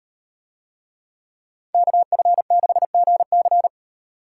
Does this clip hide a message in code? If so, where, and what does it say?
Yes: Morse code from 1.7 to 3.7 s, heard as a keyed tone, about 37 wpm, 700 Hz. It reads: KF6GC